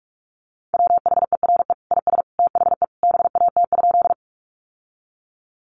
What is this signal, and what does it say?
Morse code, heard as a keyed tone, about 44 wpm, 700 Hz. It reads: WHERE IS THE BAT?
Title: Sentences from Top 1500 Words 45wpm